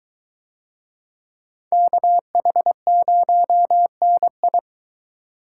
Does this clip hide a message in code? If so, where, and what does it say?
Yes: Morse code from 1.7 to 4.6 s, heard as a keyed tone, about 22 wpm, 700 Hz. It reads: KH0NI